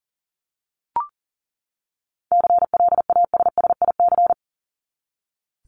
Morse code, heard as a keyed tone, about 40 wpm, 700 Hz. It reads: CLASSIC
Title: Top 751-1000 Words 40wpm